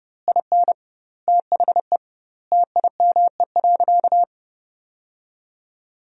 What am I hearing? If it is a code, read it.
Morse code, heard as a keyed tone, about 29 wpm, 700 Hz. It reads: IN THE TIME.
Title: 2x Word-Spacing - Sentences from Top 200 Words - 30wpm